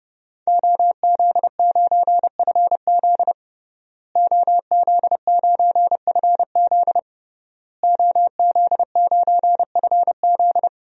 Morse code, heard as a keyed tone, about 30 wpm, 700 Hz. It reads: OZ9FZ OZ9FZ OZ9FZ
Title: DXCC-100 Call Signs - 5 Characters - Repeated 3x - 30wpm